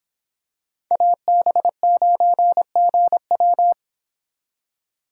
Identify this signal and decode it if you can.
Morse code, heard as a keyed tone, about 26 wpm, 700 Hz. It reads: AB9GW